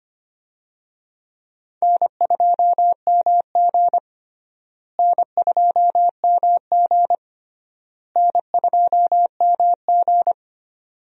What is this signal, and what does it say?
Morse code, heard as a keyed tone, about 25 wpm, 700 Hz. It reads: N2MG N2MG N2MG
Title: USA Call Signs - 4 Characters - Repeated 3x - 25wpm